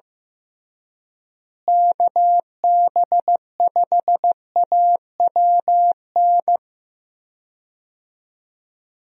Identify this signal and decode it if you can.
Morse code, heard as a keyed tone, about 15 wpm, 700 Hz. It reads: KB5AWN
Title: USA Call Signs - 6 Characters - Repeated 3x - 15wpm